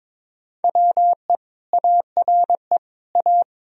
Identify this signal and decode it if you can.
Morse code, heard as a keyed tone, about 22 wpm, 700 Hz. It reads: WE ARE A
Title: Sentences from Top 1000 Words - Review Part 4 - 22wpm